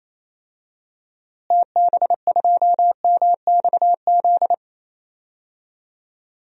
Morse code, heard as a keyed tone, about 27 wpm, 700 Hz. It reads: TB2MXZ
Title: DXCC-100 Call Signs - 6 Characters - Repeated 2x - 28wpm